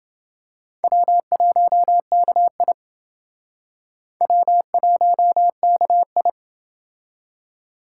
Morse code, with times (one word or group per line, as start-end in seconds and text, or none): 0.8-2.7 W1KI
4.2-6.3 W1KI